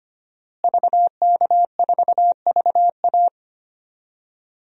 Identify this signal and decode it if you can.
Morse code, heard as a keyed tone, about 25 wpm, 700 Hz. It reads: VK4VA
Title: DXCC-100 Call Signs - 5 Characters - Repeated 3x - 25wpm